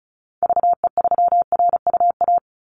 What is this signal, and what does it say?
Morse code, heard as a keyed tone, about 36 wpm, 700 Hz. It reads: VE3RUA
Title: Call Signs - 35wpm